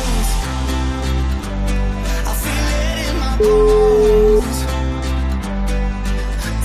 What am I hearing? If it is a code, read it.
Ringback tone (Europe, 425 Hz), with noise.